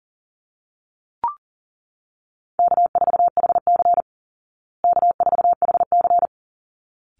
Morse code, heard as a keyed tone, about 40 wpm, 700 Hz. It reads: K4HC K4HC
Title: Easy Call Signs Repeated 2x - 40wpm